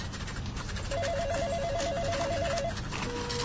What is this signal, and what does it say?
There is no code.